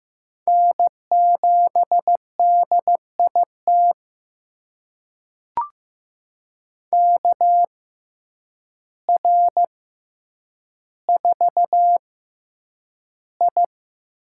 Morse code, heard as a keyed tone, about 15 wpm, spaced as for 8 wpm, 700 Hz. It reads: N7DIT KR4I
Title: USA Call Signs - 5 Characters - 6x Character Spacing - 15wpm